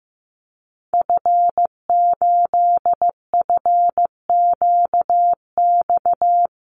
Morse code, heard as a keyed tone, about 15 wpm, 700 Hz. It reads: F8FQX
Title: Call Signs Repeated 3x - 15wpm